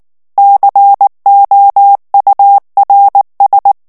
Morse code, heard as a keyed tone, about 19 wpm, 800 Hz. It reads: COURS
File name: morse mp3 / FRANCAIS TEXTE CLAIR 1 VITESSE 18 MOTS MINUTES